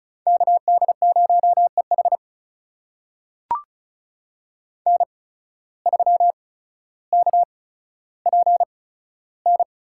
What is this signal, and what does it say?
Morse code, heard as a keyed tone, about 36 wpm, spaced as for 15 wpm, 700 Hz. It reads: KD0EH N3KPN
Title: USA Call Signs - 5 Characters - 8x Character Spacing - 35wpm